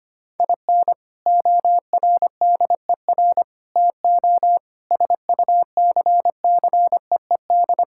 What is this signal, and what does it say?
Morse code, heard as a keyed tone, about 25 wpm, 700 Hz. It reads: IN ORDER TO SUCCEED
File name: practice / 25wpm / Sentences with 3-Word Phrases - 25wpm